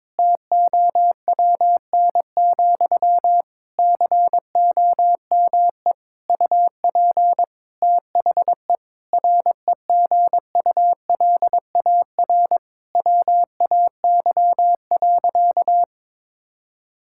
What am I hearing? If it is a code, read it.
Morse code, heard as a keyed tone, about 22 wpm, 700 Hz. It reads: TOWN, COME UP THE REGULAR WAY.
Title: Sentences from Top 1000 Words - Review Part 6 - 22wpm